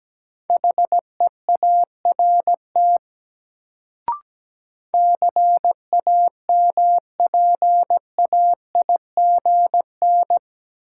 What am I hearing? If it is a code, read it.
Morse code, heard as a keyed tone, about 17 wpm, 700 Hz. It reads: HEART CAMPAIGN
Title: Sentences from Top 1200 Words - New Word Drill Alt - 17wpm